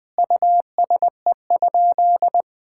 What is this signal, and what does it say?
Morse code, heard as a keyed tone, about 20 wpm, 700 Hz. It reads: USE?
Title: Sentences from Top 1000 Words - Review Part 6 - 20wpm